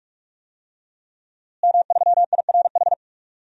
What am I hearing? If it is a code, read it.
Morse code, heard as a keyed tone, about 43 wpm, 700 Hz. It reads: M3IRH
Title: DXCC-100 Call Signs - 5 Characters - Repeated 2x - 45wpm